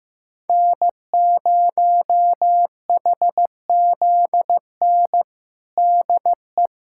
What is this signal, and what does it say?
Morse code, heard as a keyed tone, about 15 wpm, 700 Hz. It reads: N0HZN DE
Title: USA Callsign DE Callsign - 5 Characters - 15wpm